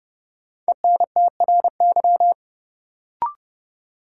Morse code, heard as a keyed tone, about 30 wpm, 700 Hz. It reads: ENTRY
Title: Sentences from Top 1500 Words - New Word Drill - 30wpm